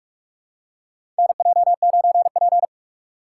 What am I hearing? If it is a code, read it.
Morse code, heard as a keyed tone, about 45 wpm, 690 Hz. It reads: NJ9P
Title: USA Call Signs - 4x Character Spacing - 45wpm